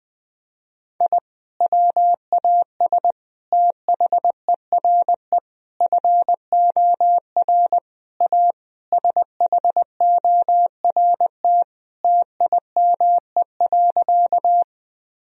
Morse code, heard as a keyed tone, about 20 wpm, 700 Hz. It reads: I WAS THERE FOR A SHORT TIME.
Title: Sentences from Top 1000 Words - Review Part 2 - 20wpm